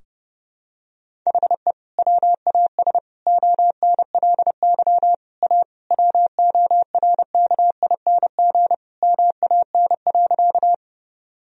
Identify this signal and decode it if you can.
Morse code, heard as a keyed tone, about 30 wpm, 700 Hz. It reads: HE WAS ONLY A WORKING MAN.
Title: Sentences from Top 1000 Words - Review Part 4 - 30wpm